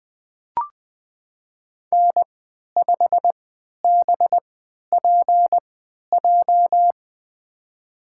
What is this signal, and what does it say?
Morse code, heard as a keyed tone, about 20 wpm, spaced as for 11 wpm, 700 Hz. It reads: N5BPJ
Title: USA Call Signs - 5 Characters - 3x Character Spacing - 20wpm